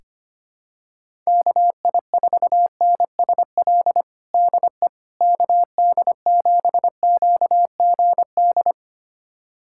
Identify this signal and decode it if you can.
Morse code, heard as a keyed tone, about 25 wpm, 700 Hz. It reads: KI4NSL DE KD7QGD
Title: USA Callsign DE Callsign - 6 Characters - 25wpm